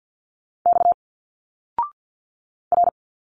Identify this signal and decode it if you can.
Morse code, heard as a keyed tone, about 50 wpm, 700 Hz. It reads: X R
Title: Single Letters 50wpm